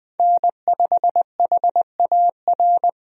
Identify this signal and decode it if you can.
Morse code, heard as a keyed tone, about 21 wpm, 700 Hz. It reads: N5HAR
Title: USA Call Signs - 5 Characters - Repeated 3x - 20wpm